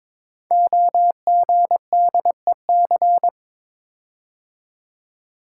Morse code, heard as a keyed tone, about 22 wpm, 700 Hz. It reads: OGDEC
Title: Code Groups - 5 Letters - 22wpm